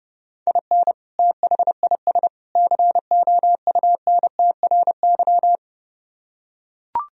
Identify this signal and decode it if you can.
Morse code, heard as a keyed tone, about 30 wpm, 700 Hz. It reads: IN THIS COUNTRY